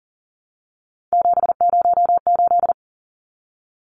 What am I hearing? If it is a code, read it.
Morse code, heard as a keyed tone, about 40 wpm, 700 Hz. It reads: 708
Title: Numbers - 3-Digit 40wpm